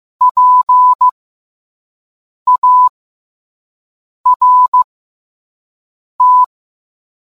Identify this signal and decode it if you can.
Morse code, heard as a keyed tone, about 15 wpm, spaced as for 5 wpm, 1000 Hz. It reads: PART